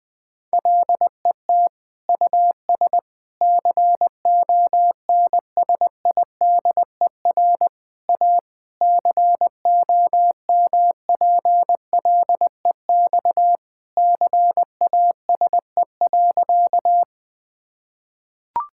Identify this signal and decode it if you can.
Morse code, heard as a keyed tone, about 20 wpm, 700 Hz. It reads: LET US CONSIDER A COMPLEX CASE.
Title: Sentences from Top 1000 Words - Review Part 6 - 20wpm